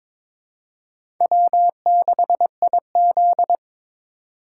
Morse code, heard as a keyed tone, about 22 wpm, 700 Hz. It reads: W6IZ
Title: USA Call Signs - 4 Characters - 2x Character Spacing - 22wpm